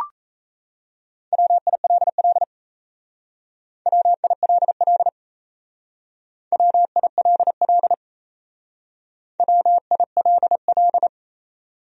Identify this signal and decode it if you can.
Morse code, heard as a keyed tone, about 34 wpm, 700 Hz. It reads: WILL WILL WILL WILL
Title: Speed-Racing Top 100 Words 28wpm